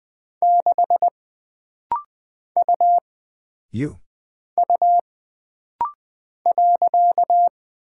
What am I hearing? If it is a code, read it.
Morse code, heard as a keyed tone, about 20 wpm, 700 Hz. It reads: 6 U U .
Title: Single Character (Letter:Number:Punctuation:Prosign) - Rapid-Fire - 20wpm